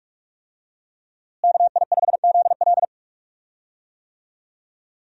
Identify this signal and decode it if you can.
Morse code, heard as a keyed tone, about 42 wpm, 700 Hz. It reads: KI5ZL
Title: USA Call Signs - 5 Characters - Repeated 2x - 45wpm